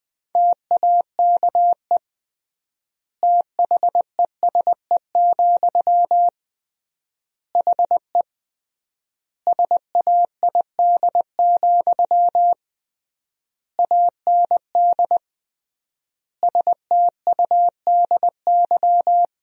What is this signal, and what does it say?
Morse code, heard as a keyed tone, about 20 wpm, 700 Hz. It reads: TAKE THESE, HE SAID, AND STUDY
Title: 3x Word-Spacing - Sentences from Top 300 Words - 20wpm